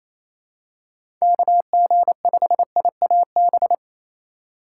Morse code, heard as a keyed tone, about 28 wpm, 700 Hz. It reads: KG5IAB
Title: USA Call Signs - 6 Characters - 4x Character Spacing - 28wpm